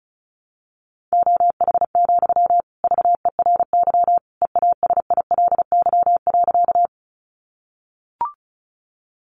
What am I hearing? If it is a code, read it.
Morse code, heard as a keyed tone, about 34 wpm, 700 Hz. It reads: OH, VERY EASILY.
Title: Sentences from Top 900 Words 35wpm